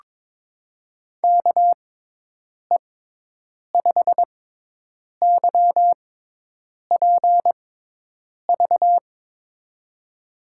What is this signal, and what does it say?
Morse code, heard as a keyed tone, about 21 wpm, spaced as for 8 wpm, 700 Hz. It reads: KE5YPV
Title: USA Call Signs - 6 Characters - 6x Character Spacing - 22wpm